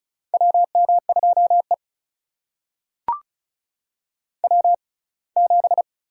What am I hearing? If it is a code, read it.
Morse code, heard as a keyed tone, about 34 wpm, spaced as for 23 wpm, 700 Hz. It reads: WM2E W7
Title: USA Call Signs - 6x Character Spacing - 35wpm